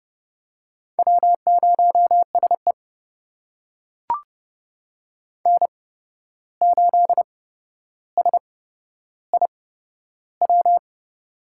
Morse code, heard as a keyed tone, about 29 wpm, spaced as for 12 wpm, 700 Hz. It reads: W0SE N8SIW
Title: USA Call Signs - 8x Character Spacing - 30wpm